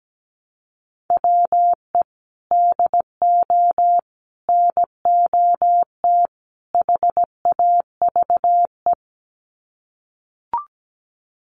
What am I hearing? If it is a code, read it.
Morse code, heard as a keyed tone, about 17 wpm, 700 Hz. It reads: WE DO NOT HAVE
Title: Sets of 4 Words 17wpm